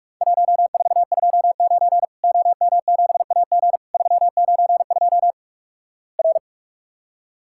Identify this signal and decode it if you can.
Morse code, heard as a keyed tone, about 46 wpm, 700 Hz. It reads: 1429 OM7AG 392 R
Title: Worked All Europe DX Contest - QTC Exchange - 45wpm